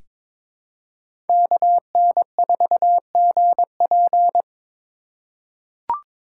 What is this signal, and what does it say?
Morse code, heard as a keyed tone, about 22 wpm, 700 Hz. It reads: KN4GP
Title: USA Call Signs - 5 Characters - 3x Character Spacing - 22wpm